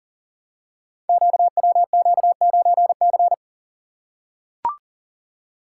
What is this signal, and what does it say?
Morse code, heard as a keyed tone, about 40 wpm, 700 Hz. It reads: QWQ9C
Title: Code Groups - 5 Letter:Numbers - 40wpm